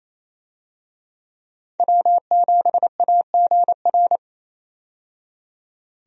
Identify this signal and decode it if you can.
Morse code, heard as a keyed tone, about 28 wpm, 700 Hz. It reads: W7AGR
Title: USA Call Signs - 5 Characters - 28wpm